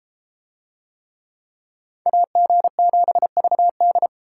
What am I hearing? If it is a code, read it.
Morse code, heard as a keyed tone, about 31 wpm, 700 Hz. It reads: AG7VD